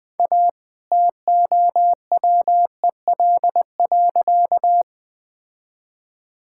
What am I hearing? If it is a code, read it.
Morse code, heard as a keyed tone, about 20 wpm, 700 Hz. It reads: A TOWEL.
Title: All-Age Appropriate Jokes - 20wpm